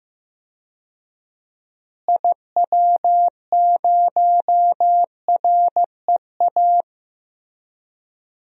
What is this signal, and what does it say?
Morse code, heard as a keyed tone, about 15 wpm, 700 Hz. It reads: IW0REA
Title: DXCC-100 Call Signs - 6 Characters - Repeated 2x - 15wpm